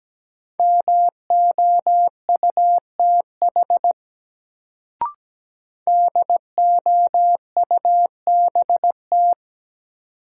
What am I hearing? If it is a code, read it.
Morse code, heard as a keyed tone, about 17 wpm, 700 Hz. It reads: MOUTH DOUBT
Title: Sentences from Top 1500 Words - New Word Drill Alt - 17wpm